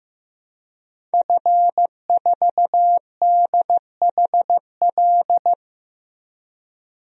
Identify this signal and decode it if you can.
Morse code, heard as a keyed tone, about 15 wpm, 700 Hz. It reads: F4DHL
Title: DXCC-100 Call Signs - 5 Characters - 15wpm